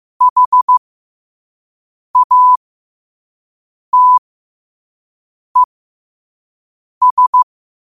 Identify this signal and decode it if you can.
Morse code, heard as a keyed tone, about 15 wpm, spaced as for 5 wpm, 1000 Hz. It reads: HATES
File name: qotdfiles / 5wpm